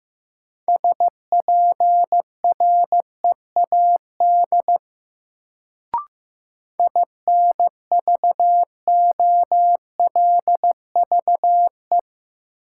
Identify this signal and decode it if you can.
Morse code, heard as a keyed tone, about 15 wpm, 700 Hz. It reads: SPREAD INVOLVE
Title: Sentences from Top 1200 Words - New Word Drill Alt - 15wpm